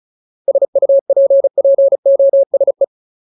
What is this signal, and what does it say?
Morse code, heard as a keyed tone, about 34 wpm, 550 Hz. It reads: SUPPOSE
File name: Ultimate Word List / 35WPM Ultimate